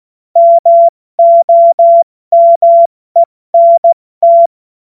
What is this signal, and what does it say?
Morse code, heard as a keyed tone, about 16 wpm, 680 Hz. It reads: MOMENT
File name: Morsekurs / Witze / Witz 03 Juni Mai 2025 70 F80